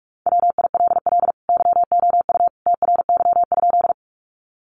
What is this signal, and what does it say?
Morse code, heard as a keyed tone, about 44 wpm, 700 Hz. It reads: WILL YOU TRY?
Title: Sentences from Top 200 Words 45wpm